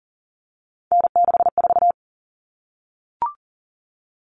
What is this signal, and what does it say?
Morse code, heard as a keyed tone, about 40 wpm, 700 Hz. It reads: N64
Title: Common Cut Numbers - 3-Digit 40wpm